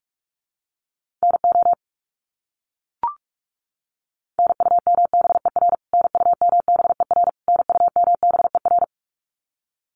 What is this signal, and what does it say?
Morse code, heard as a keyed tone, about 44 wpm, 700 Hz. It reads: NO NUMBER NUMBER NUMBER
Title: Top 300 Words Repeated 45wpm